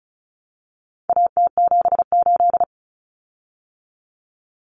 Morse code, heard as a keyed tone, about 34 wpm, 700 Hz. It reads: AT78
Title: Common Cut Numbers - 4-Digit 35wpm